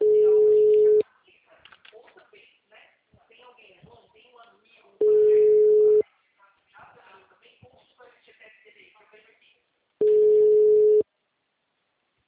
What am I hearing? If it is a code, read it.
Ringback tone (Europe, 425 Hz), with speech.
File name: out- -2006- - - .45397